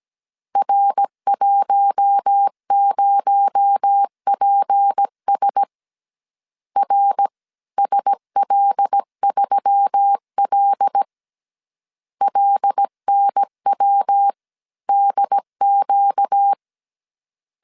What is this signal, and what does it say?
Morse code, heard as a keyed tone, about 17 wpm, 800 Hz. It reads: R10PS RSL3L LNWDQ
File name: CW / DET / 1 6 1500C S A 16wpm 2635